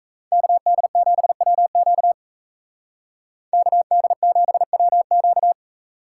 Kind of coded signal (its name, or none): Morse code